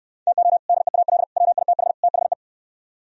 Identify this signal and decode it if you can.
Morse code, heard as a keyed tone, about 67 wpm, 700 Hz. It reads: TO DEAL WITH THE